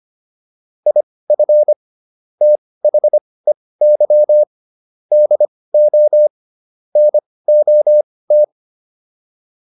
Morse code, heard as a keyed tone, about 25 wpm, spaced as for 18 wpm, 600 Hz. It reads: IF THEY DO NOT